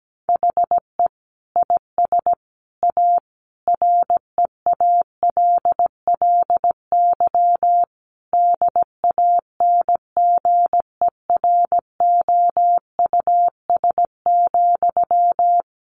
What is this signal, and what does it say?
Morse code, heard as a keyed tone, about 17 wpm, 700 Hz. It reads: HE IS A REALLY DANGEROUS,
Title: Sentences from Top 1200 Words 17wpm